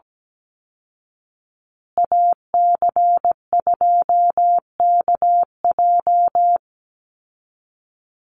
Morse code, heard as a keyed tone, about 17 wpm, 700 Hz. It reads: AC2KJ
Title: Call Signs Repeated 2x - 17wpm